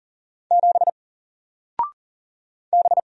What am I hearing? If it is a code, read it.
Morse code, heard as a keyed tone, about 39 wpm, 700 Hz. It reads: 7 B
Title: Single Character (Letter:Number:Punctuation:Prosign) 40wpm